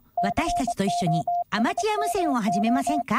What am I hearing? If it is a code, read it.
Morse code, heard as a keyed tone, about 23 wpm, 700 Hz, with speech. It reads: COSMO